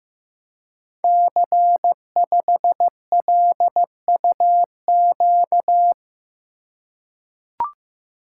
Morse code, heard as a keyed tone, about 15 wpm, 700 Hz. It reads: C5LUQ
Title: Code Groups - 5 Letter:Numbers - 15wpm